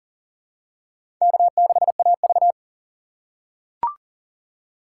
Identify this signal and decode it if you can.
Morse code, heard as a keyed tone, about 40 wpm, 700 Hz. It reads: K6AV
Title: USA Call Signs - 4 Characters - 6x Character Spacing - 40wpm